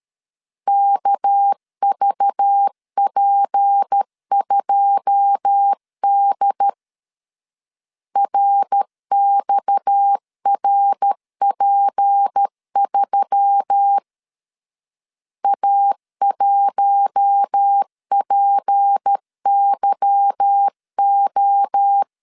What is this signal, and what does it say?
Morse code, heard as a keyed tone, about 13 wpm, 800 Hz. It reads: KVP2D RXRP3 A1PYO
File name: LEZIONI / 9 2 12wpm 2301